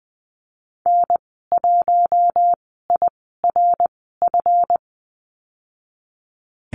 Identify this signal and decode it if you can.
Morse code, heard as a keyed tone, about 20 wpm, spaced as for 15 wpm, 700 Hz. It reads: N1IRF
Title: 173 Lesson 28g - Number 9 - Call Signs 20wpm